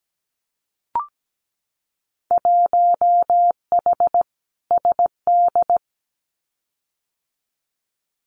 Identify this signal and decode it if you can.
Morse code, heard as a keyed tone, about 17 wpm, 700 Hz. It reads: 1H SD
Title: Winter Field Day Exchanges 17wpm